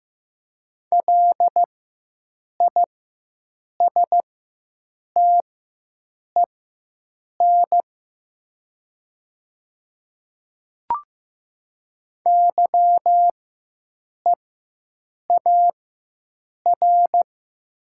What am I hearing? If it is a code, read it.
Morse code, heard as a keyed tone, about 15 wpm, spaced as for 7 wpm, 700 Hz. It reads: LISTEN YEAR
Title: 4x Character Spacing - Top 500 Words - 15wpm